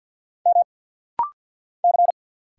Morse code, heard as a keyed tone, about 50 wpm, 690 Hz. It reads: M C
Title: Single Letter-Number Alt - Rapid-Fire 50wpm